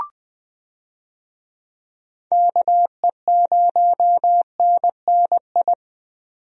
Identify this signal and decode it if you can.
Morse code, heard as a keyed tone, about 20 wpm, 700 Hz. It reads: KE0NNI
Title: USA Call Signs - 6 Characters - 20wpm